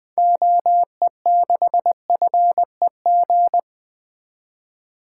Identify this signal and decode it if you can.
Morse code, heard as a keyed tone, about 21 wpm, 700 Hz. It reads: OE6FEG
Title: 2023 SOTA Activator Call Signs - 20wpm